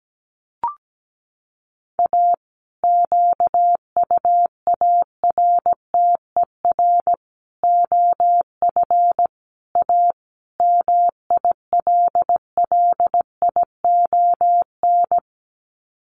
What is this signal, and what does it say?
Morse code, heard as a keyed tone, about 17 wpm, 700 Hz. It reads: A QUARTER OF A MILLION
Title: Sets of 5 Words Encore2 17wpm